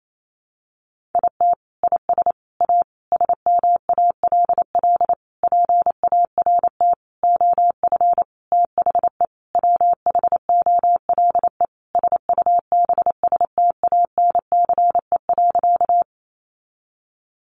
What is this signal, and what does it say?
Morse code, heard as a keyed tone, about 28 wpm, 700 Hz. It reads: IT IS A SMALL PART OF THE WHOLE SUBSTANCE.
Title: Sentences from Top 1500 Words 28wpm